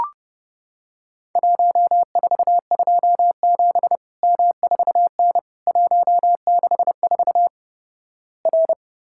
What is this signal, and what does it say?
Morse code, heard as a keyed tone, about 30 wpm, 700 Hz. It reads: 1427 M4N 164 R
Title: Worked All Europe DX Contest - QTC Exchange - 30wpm